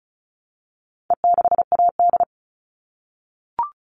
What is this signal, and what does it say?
Morse code, heard as a keyed tone, about 34 wpm, 700 Hz. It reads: E6AD
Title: All Cut Numbers - 4-Digit 35wpm